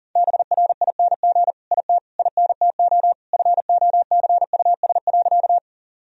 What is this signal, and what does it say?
Morse code, heard as a keyed tone, about 40 wpm, 700 Hz. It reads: BRING IT INTO FOCUS.
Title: Sentences from Top 1000 Words - Review Part 4 - 40wpm